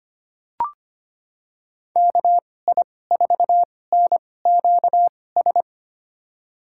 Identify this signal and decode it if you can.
Morse code, heard as a keyed tone, about 25 wpm, spaced as for 18 wpm, 700 Hz. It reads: KI4NQS
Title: USA Call Signs - 2x Character Spacing - 25wpm